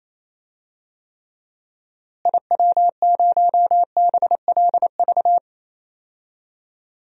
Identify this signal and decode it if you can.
Morse code, heard as a keyed tone, about 27 wpm, 700 Hz. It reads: IW0BLV
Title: DXCC-100 Call Signs - 6 Characters - Repeated 2x - 28wpm